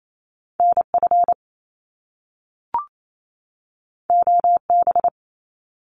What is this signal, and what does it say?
Morse code, heard as a keyed tone, about 28 wpm, 700 Hz. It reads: N F O B
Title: Top Two Letter Combinations in Words 28wpm